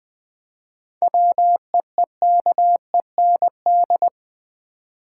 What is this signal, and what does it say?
Morse code, heard as a keyed tone, about 20 wpm, 700 Hz. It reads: WEEKEND
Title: Seven Letter Words - 20wpm